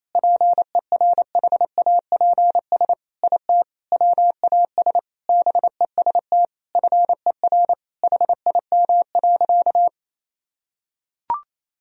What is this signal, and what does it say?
Morse code, heard as a keyed tone, about 28 wpm, 700 Hz. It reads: PERHAPS IT WAS BEST FER HIM.
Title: Sentences from Top 500 Words and Abbreviations 28wpm